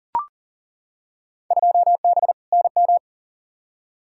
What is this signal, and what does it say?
Morse code, heard as a keyed tone, about 39 wpm, 700 Hz. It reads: 2B NM
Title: Field Day Exchanges Encore - 40wpm